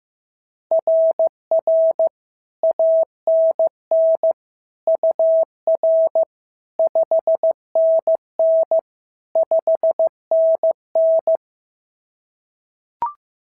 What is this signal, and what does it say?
Morse code, heard as a keyed tone, about 14 wpm, 650 Hz. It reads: RR ANN UR 5NN 5NN
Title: SOTA QSO Practice 15wpm